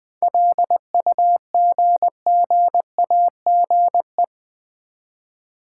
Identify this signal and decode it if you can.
Morse code, heard as a keyed tone, about 20 wpm, 700 Hz. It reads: LUGGAGE
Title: Easy Words with Hints - 20wpm